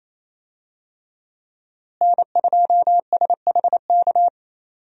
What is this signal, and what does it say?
Morse code, heard as a keyed tone, about 28 wpm, 700 Hz. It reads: N2SHK
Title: USA Call Signs - 5 Characters - Repeated 3x - 28wpm